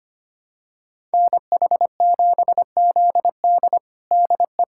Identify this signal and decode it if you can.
Morse code, heard as a keyed tone, about 25 wpm, 700 Hz. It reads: NH7ZD DE
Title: USA Callsign DE Callsign - 5 Characters - 25wpm